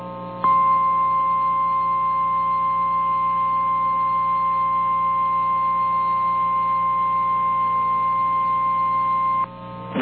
Emergency Alert System (SAME) tones (the 1050 Hz attention tone). It is heard from 0.4 to 9.5 s.
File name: eas / KNCYKBIE / KNCYKBIE